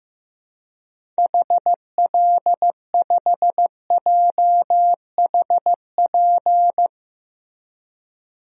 Morse code, heard as a keyed tone, about 16 wpm, 700 Hz. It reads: HL5JHP
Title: DXCC-100 Call Signs - 6 Characters - Repeated 2x - 15wpm